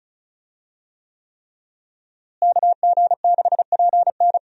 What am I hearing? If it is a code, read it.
Morse code, heard as a keyed tone, about 36 wpm, 700 Hz. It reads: KG6PN
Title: USA Call Signs - 5 Characters - Repeated 3x - 35wpm